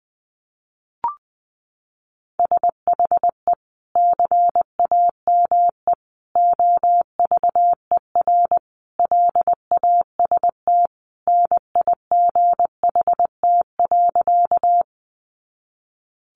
Morse code, heard as a keyed tone, about 20 wpm, 700 Hz. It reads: SHE CAME OVER LAST NIGHT.
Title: Sentences from Top 300 Words 20wpm